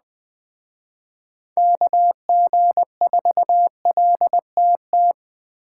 Morse code, heard as a keyed tone, about 21 wpm, 700 Hz. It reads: KG4LTT
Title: Speed Racing - USA 2x3 Call Signs 17wpm